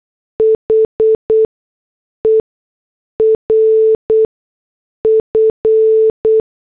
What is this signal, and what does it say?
Morse code, heard as a keyed tone, about 8 wpm, 440 Hz. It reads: H E R F
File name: morse / headlines